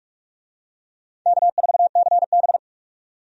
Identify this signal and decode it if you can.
Morse code, heard as a keyed tone, about 46 wpm, 700 Hz. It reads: K4CB
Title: USA Call Signs - 4 Characters - 45wpm